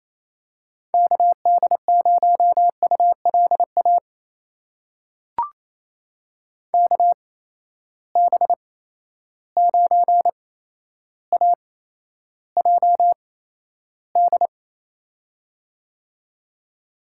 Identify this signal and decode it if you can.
Morse code, heard as a keyed tone, about 28 wpm, spaced as for 12 wpm, 700 Hz. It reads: KD0ULA KB9AJD
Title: USA Call Signs - 6 Characters - 8x Character Spacing - 28wpm